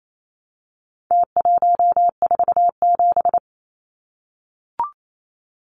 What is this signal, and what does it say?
Morse code, heard as a keyed tone, about 28 wpm, 700 Hz. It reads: T147
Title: Numbers - Contest Style - 4-Digit 28wpm